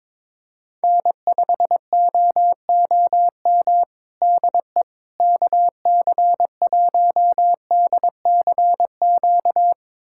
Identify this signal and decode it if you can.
Morse code, heard as a keyed tone, about 22 wpm, 700 Hz. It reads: N5OOM DE KC1DCQ
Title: USA Callsign DE Callsign 22wpm